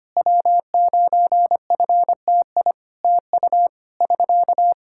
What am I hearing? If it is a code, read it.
Morse code, heard as a keyed tone, about 25 wpm, 700 Hz. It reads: W9FTI TU <SK>
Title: Practice QSO - Exchange 4 - Ending - 25wpm